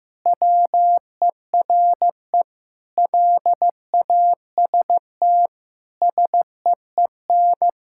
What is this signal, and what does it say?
Morse code, heard as a keyed tone, about 15 wpm, 700 Hz. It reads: WERE LAST SEEN
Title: Pangram Challenge v2 - 15wpm